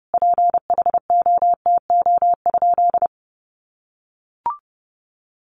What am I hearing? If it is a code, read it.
Morse code, heard as a keyed tone, about 30 wpm, 700 Hz. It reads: PHOTO?
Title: Sentences from Top 1200 Words 30wpm